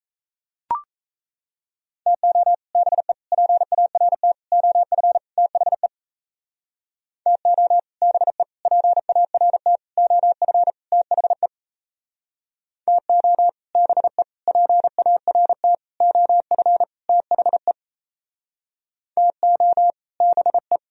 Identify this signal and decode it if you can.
Morse code, heard as a keyed tone, about 37 wpm, 700 Hz. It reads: TO BE PART OF THE TO BE PART OF THE TO BE PART OF THE TO BE